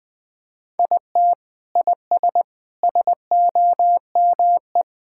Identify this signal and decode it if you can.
Morse code, heard as a keyed tone, about 21 wpm, 700 Hz. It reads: IT IS SOME
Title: Sentences with 3-Word Phrases - Spoken Beginning - 20wpm